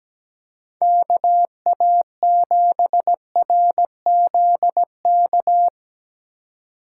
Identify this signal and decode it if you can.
Morse code, heard as a keyed tone, about 19 wpm, 700 Hz. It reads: KA7RZK
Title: Speed Racing - USA 2x3 Call Signs 17wpm